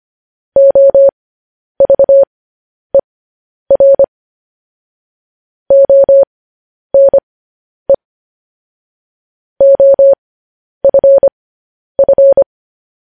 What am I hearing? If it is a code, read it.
Morse code, heard as a keyed tone, about 25 wpm, spaced as for 10 wpm, 550 Hz. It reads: OVER ONE OFF
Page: news-headlines-in-morse-code-at-10-wpm